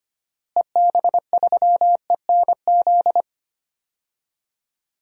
Morse code, heard as a keyed tone, about 25 wpm, 700 Hz. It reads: EB3ENZ
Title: DXCC-100 Call Signs - 6 Characters - Repeated 2x - 25wpm